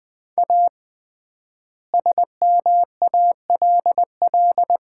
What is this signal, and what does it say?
Morse code, heard as a keyed tone, about 21 wpm, 700 Hz. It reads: A SMALL
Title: 3x Word-Spacing - Sets of 2 Words - 20wpm